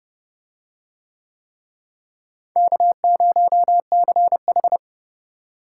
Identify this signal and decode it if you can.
Morse code, heard as a keyed tone, about 28 wpm, 700 Hz. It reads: K0CH